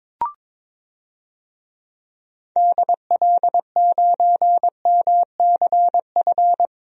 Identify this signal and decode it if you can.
Morse code, heard as a keyed tone, about 22 wpm, 700 Hz. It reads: DL9MCF
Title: DXCC-100 Call Signs - 6 Characters - Repeated 3x - 22wpm